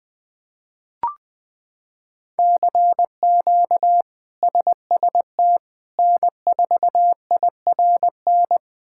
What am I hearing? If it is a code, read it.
Morse code, heard as a keyed tone, about 20 wpm, 700 Hz. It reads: CQ SST N4IRN
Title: K1USN SST QSO Practice - 20wpm